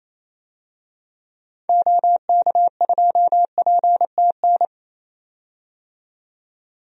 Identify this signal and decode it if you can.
Morse code, heard as a keyed tone, about 28 wpm, 700 Hz. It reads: OK2PTN
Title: DXCC-100 Call Signs - 6 Characters - 28wpm